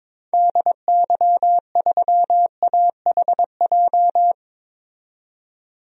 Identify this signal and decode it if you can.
Morse code, heard as a keyed tone, about 22 wpm, 700 Hz. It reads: DY3AHJ
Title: DXCC-100 Call Signs - 6 Characters - 22wpm